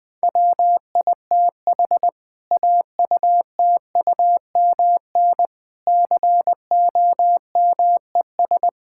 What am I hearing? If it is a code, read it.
Morse code, heard as a keyed tone, about 20 wpm, 700 Hz. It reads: WITH AUTUMN COMES
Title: Difficult Sentences with Hints - 20wpm